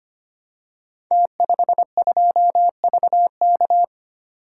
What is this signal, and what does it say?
Morse code, heard as a keyed tone, about 25 wpm, 700 Hz. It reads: T52VK
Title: Code Groups - 5 Letter:Numbers - 25wpm